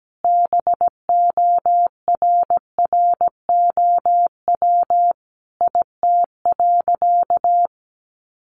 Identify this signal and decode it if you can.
Morse code, heard as a keyed tone, about 17 wpm, 700 Hz. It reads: BORROW IT.
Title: Sentences from Top 1500 Words 17wpm